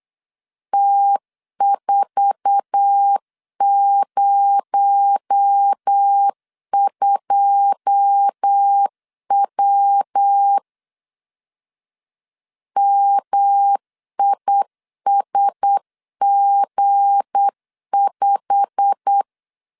Morse code, heard as a keyed tone, about 8 wpm, 800 Hz. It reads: T402W MISG5
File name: LEZIONI / 9 2 8wpm 3437